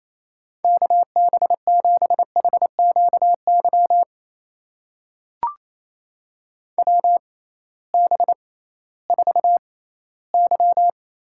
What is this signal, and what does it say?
Morse code, heard as a keyed tone, about 28 wpm, spaced as for 16 wpm, 700 Hz. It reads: KB7HQY WB4Y
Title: USA Call Signs - 6 Characters - 6x Character Spacing - 28wpm